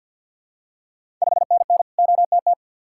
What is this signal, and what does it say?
Morse code, heard as a keyed tone, about 48 wpm, 700 Hz. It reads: 5NN OTT
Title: Ontario QSO Party Exchange 50wpm